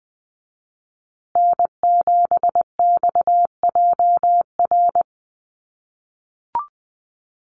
Morse code, heard as a keyed tone, about 20 wpm, 700 Hz. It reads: N7XJR
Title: USA Call Signs Repeated 3x 20wpm